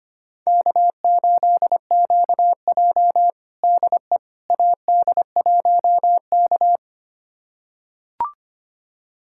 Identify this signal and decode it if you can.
Morse code, heard as a keyed tone, about 25 wpm, 700 Hz. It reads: K8QJ DE AD1K
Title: USA Callsign DE Callsign - 4 Characters - 25wpm